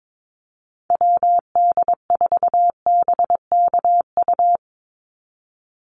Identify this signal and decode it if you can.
Morse code, heard as a keyed tone, about 22 wpm, 700 Hz. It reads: WD4BKU
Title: USA Call Signs 22wpm